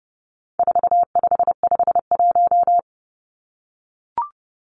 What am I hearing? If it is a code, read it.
Morse code, heard as a keyed tone, about 30 wpm, 700 Hz. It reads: 4551
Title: Numbers - 4-Digit 30wpm